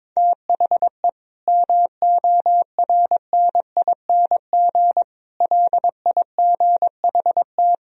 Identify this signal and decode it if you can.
Morse code, heard as a keyed tone, about 22 wpm, 700 Hz. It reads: THE MORNING LIGHT
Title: Sentences from Top 1000 Words - Review Part 1 - 22wpm